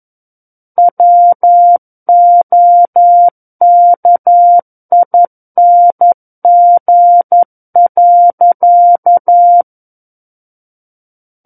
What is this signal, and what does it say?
Morse code, heard as a keyed tone, about 11 wpm, 700 Hz. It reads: WOKING.
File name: Book One / 02-Chapter 2 - 11 WPM